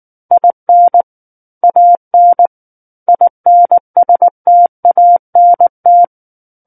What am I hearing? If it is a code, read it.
Morse code, heard as a keyed tone, about 19 wpm, 700 Hz. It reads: IN AN INSTANT
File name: Book One / 10-Chapter 10 - 19 WPM